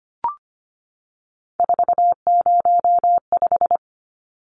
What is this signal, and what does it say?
Morse code, heard as a keyed tone, about 25 wpm, 700 Hz. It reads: 405
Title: Numbers - 3-Digit 25wpm